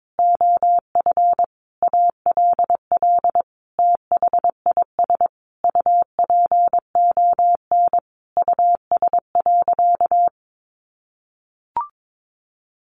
Morse code, heard as a keyed tone, about 22 wpm, 700 Hz. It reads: OF ALL THIS UPON US.
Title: Sentences from Top 1400 Words 22wpm